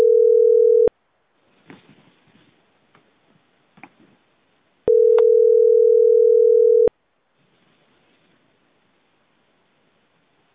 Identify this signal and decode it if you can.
Ringback tone (North America), 440 + 480 Hz.